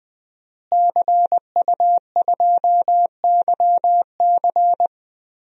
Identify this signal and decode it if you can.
Morse code, heard as a keyed tone, about 20 wpm, 700 Hz. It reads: CU2YC
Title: DXCC-100 Call Signs - 5 Characters - Repeated 2x - 20wpm